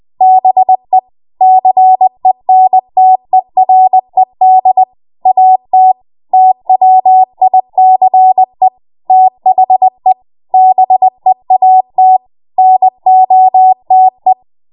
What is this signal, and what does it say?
Morse code, heard as a keyed tone, about 20 wpm, 750 Hz. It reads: BE CENTERED AT TWICE THE BEAT NOTE
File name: Morse / 20WPM